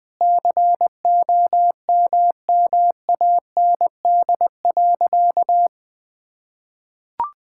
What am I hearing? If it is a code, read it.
Morse code, heard as a keyed tone, about 20 wpm, 700 Hz. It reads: COMMAND.
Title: Sentences from Top 1600 Words 20wpm